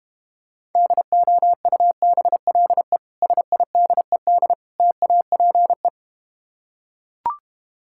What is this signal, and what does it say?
Morse code, heard as a keyed tone, about 32 wpm, 700 Hz. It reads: DOUBLE SIDED TAPE